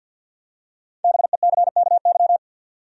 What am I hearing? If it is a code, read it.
Morse code, heard as a keyed tone, about 50 wpm, 700 Hz. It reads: BECKY
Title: USA Common Names - Max 5 Letters - 50wpm